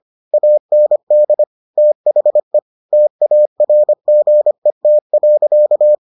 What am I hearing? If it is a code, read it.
Morse code, heard as a keyed tone, about 25 wpm, 600 Hz. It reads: AND THE TARGET.